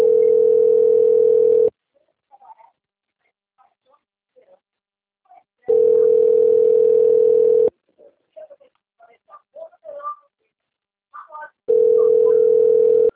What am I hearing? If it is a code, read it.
Ringback tone (North America), 440 + 480 Hz, with speech.
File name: rg-1600-unknown- - - .52570